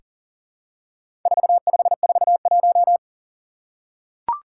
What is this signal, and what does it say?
Morse code, heard as a keyed tone, about 37 wpm, 700 Hz. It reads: 4541
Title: Four Challenging Letter-Number Combinations 40wpm